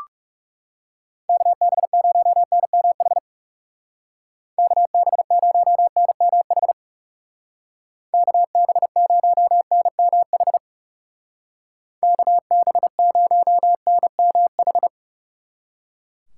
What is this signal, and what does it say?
Morse code, heard as a keyed tone, about 37 wpm, 700 Hz. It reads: KB0NMH KB0NMH KB0NMH KB0NMH
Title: Speed Racing - USA 2x3 Call Signs 30wpm